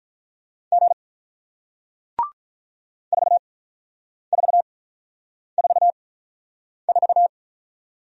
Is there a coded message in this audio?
Yes: Morse code from 0.7 to 7.3 s, heard as a keyed tone, about 48 wpm, spaced as for 9 wpm, 700 Hz. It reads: G 4444